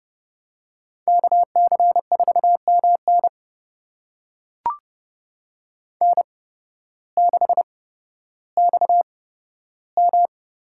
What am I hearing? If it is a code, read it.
Morse code, heard as a keyed tone, about 30 wpm, spaced as for 14 wpm, 700 Hz. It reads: KC4MN N6XM
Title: USA Call Signs - 8x Character Spacing - 30wpm